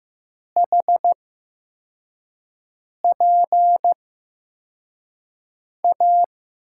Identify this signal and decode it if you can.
Morse code, heard as a keyed tone, about 15 wpm, spaced as for 4 wpm, 700 Hz. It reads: HPA
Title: USA Call Signs - 8x Character Spacing - 15wpm